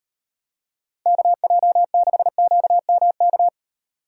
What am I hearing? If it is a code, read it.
Morse code, heard as a keyed tone, about 38 wpm, 700 Hz. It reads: KJ6QMK